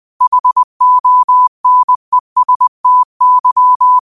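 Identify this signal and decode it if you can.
Morse code, heard as a keyed tone, about 20 wpm, 1000 Hz. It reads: HONESTY
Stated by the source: 20 WPM Morse code quotes, QOTD at 20 WPM.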